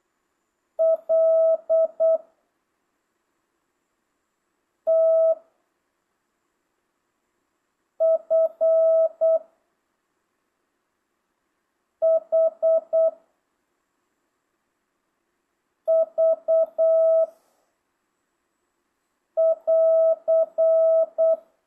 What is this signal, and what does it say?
Morse code, heard as a keyed tone, about 8 wpm, 660 Hz. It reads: L T F H V <AR>
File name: jh8cbh / 2025cw